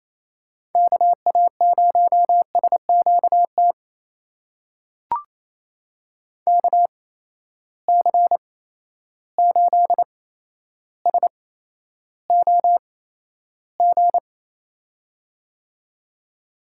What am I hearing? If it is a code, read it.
Morse code, heard as a keyed tone, about 28 wpm, spaced as for 12 wpm, 700 Hz. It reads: KA0SQT KC8SOG
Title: USA Call Signs - 6 Characters - 8x Character Spacing - 28wpm